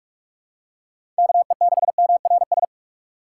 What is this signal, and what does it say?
Morse code, heard as a keyed tone, about 46 wpm, 700 Hz. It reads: KE6MRS
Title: USA Call Signs - 4x Character Spacing - 45wpm